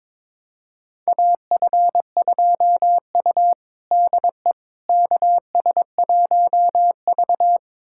Morse code, heard as a keyed tone, about 22 wpm, 700 Hz. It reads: AF2U DE KS1V
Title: USA Callsign DE Callsign - 4 Characters - 22wpm